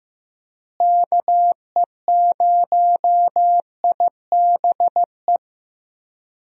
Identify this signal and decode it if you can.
Morse code, heard as a keyed tone, about 15 wpm, 700 Hz. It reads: KE0IBE
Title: USA Call Signs - 6 Characters - Repeated 3x - 15wpm